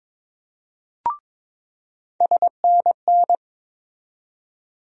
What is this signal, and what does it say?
Morse code, heard as a keyed tone, about 22 wpm, 700 Hz. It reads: SNN
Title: USA Call Sign Postfixes - 3 Letters 22wpm